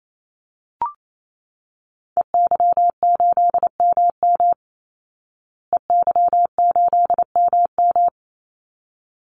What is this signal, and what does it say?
Morse code, heard as a keyed tone, about 28 wpm, 700 Hz. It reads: EY8MM EY8MM
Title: Call Signs Repeated 3x - 28wpm